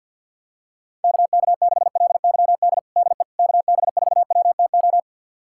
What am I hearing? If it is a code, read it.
Morse code, heard as a keyed tone, about 47 wpm, 700 Hz. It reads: KK6LYD DE KB4WTO
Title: USA Callsign DE Callsign - 6 Characters - 50wpm